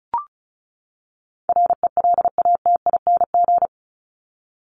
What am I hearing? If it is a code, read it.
Morse code, heard as a keyed tone, about 34 wpm, 700 Hz. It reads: RELATING
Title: ING Suffix - Top 100 Words with Base Words 35wpm